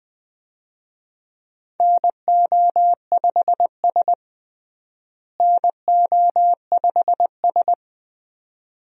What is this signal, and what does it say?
Morse code, heard as a keyed tone, about 20 wpm, 700 Hz. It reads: NO5S NO5S
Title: USA Call Signs - 4 Characters - Repeated 2x - 20wpm